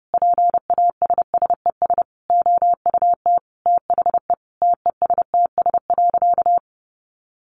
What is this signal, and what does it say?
Morse code, heard as a keyed tone, about 30 wpm, 700 Hz. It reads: PASSES OUT THE TESTS.
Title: Additional Easy Phrases 30wpm